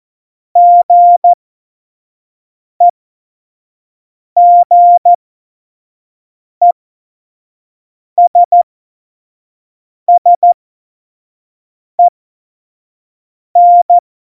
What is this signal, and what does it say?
Morse code, heard as a keyed tone, about 14 wpm, spaced as for 5 wpm, 700 Hz. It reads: GEGESSEN
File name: Morsekurs / Witze / Witz 14. Januar 2020, 25 F70